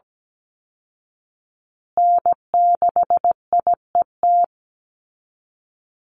Morse code, heard as a keyed tone, about 17 wpm, 700 Hz. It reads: N6IET
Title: Call Signs Repeated 2x - 17wpm